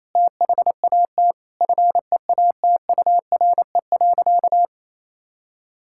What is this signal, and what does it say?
Morse code, heard as a keyed tone, about 28 wpm, 700 Hz. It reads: THAT FEATURE.
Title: Sentences from Top 1000 Words - Review Part 5 - 28wpm